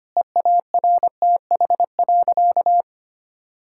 Morse code, heard as a keyed tone, about 25 wpm, 700 Hz. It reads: EARTH.
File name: practice / 25wpm / Difficult Sentences with Hints - 25wpm